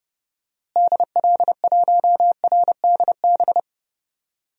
Morse code, heard as a keyed tone, about 32 wpm, 700 Hz. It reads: DL1RDB
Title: DXCC-100 Call Signs - 6 Characters - Repeated 3x - 30wpm